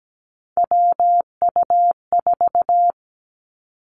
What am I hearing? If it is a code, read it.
Morse code, heard as a keyed tone, about 17 wpm, 700 Hz. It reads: WU4
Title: Three Challenging Letter-Number Combinations 17wpm